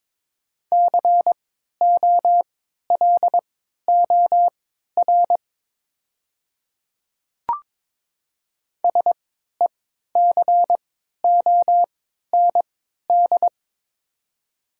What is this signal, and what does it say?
Morse code, heard as a keyed tone, about 22 wpm, spaced as for 13 wpm, 700 Hz. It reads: COLOR SECOND
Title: 3x Character Spacing - Top 300 Words - 22wpm